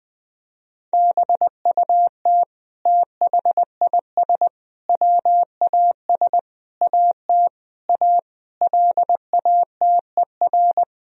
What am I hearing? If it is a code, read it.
Morse code, heard as a keyed tone, about 20 wpm, 700 Hz. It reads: BUT THIS WAS AT A LATER
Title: Sentences from Top 1000 Words - Review Part 4 - 20wpm